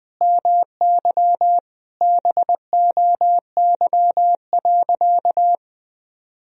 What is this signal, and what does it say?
Morse code, heard as a keyed tone, about 20 wpm, 700 Hz. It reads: MY BOY.
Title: Sentences from Top 1000 Words - Review Part 1 - 20wpm